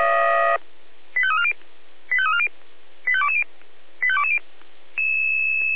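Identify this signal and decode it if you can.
Five-tone selective call (ZVEI), heard as quick runs of five tones, tones of about 70 ms. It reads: 85340 85340 85220 85220
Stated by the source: ZVEI five-tone call received by SDR.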